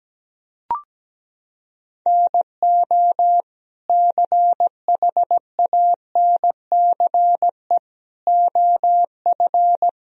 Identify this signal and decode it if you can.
Morse code, heard as a keyed tone, about 17 wpm, 700 Hz. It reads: NO CHANCE OF
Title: Sentences from Top 1000 Words - Review Part 2 - 17wpm